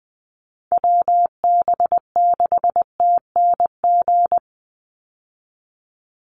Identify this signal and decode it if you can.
Morse code, heard as a keyed tone, about 20 wpm, 700 Hz. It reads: WB6TNG
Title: USA Call Signs 20wpm